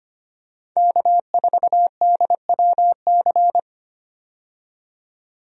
Morse code, heard as a keyed tone, about 25 wpm, 700 Hz. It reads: K4DWC
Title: USA Call Signs - 5 Characters - 25wpm